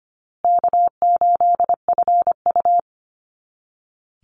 Morse code, heard as a keyed tone, about 25 wpm, 700 Hz. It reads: K8FU
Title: Easy Call Signs Repeated 3x - 25wpm